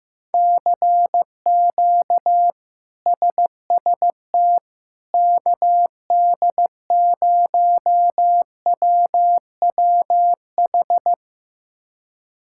Morse code, heard as a keyed tone, about 16 wpm, 700 Hz. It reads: CQ SST KD0WWH
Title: K1USN SST QSO Practice - 15wpm